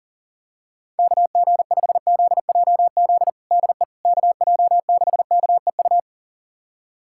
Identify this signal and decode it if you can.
Morse code, heard as a keyed tone, about 40 wpm, 700 Hz. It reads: KG5ZJZ DE KJ6KEU
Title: USA Callsign DE Callsign - 6 Characters - 40wpm